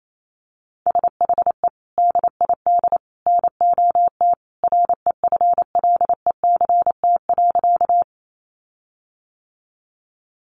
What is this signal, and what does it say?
Morse code, heard as a keyed tone, about 28 wpm, 700 Hz. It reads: SHE DID NOT REFLECT.
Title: Sentences from Top 1500 Words 28wpm